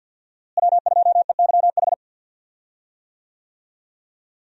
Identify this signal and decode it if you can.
Morse code, heard as a keyed tone, about 52 wpm, 700 Hz. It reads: W2EYH
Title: USA Call Signs - 5 Characters - Repeated 2x - 50wpm